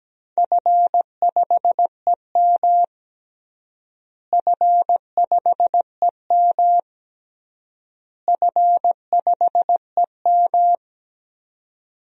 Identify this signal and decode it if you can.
Morse code, heard as a keyed tone, about 17 wpm, 700 Hz. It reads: F5EM F5EM F5EM
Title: DXCC-100 Call Signs - 4 Characters - Repeated 3x - 17wpm